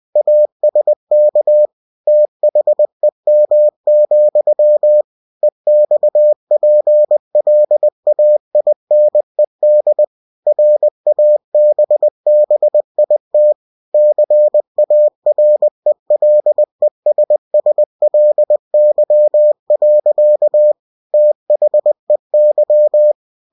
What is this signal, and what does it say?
Morse code, heard as a keyed tone, about 20 wpm, 600 Hz. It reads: ASK THEM, EXPLAINED RABBIT CARELESSLY. THEY
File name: cw-books / winnie-the-pooh / winnie-20wpm-page-0079